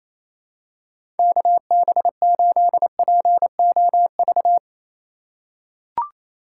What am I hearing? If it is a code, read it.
Morse code, heard as a keyed tone, about 28 wpm, 700 Hz. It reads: KB8POV
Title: USA Call Signs - 6 Characters - 4x Character Spacing - 28wpm